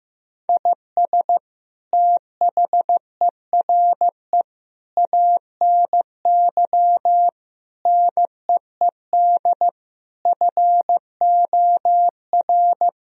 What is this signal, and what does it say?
Morse code, heard as a keyed tone, about 15 wpm, 700 Hz. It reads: IS THERE ANY NEED FOR
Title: Sentences from Top 1000 Words - Review Part 4 - 15wpm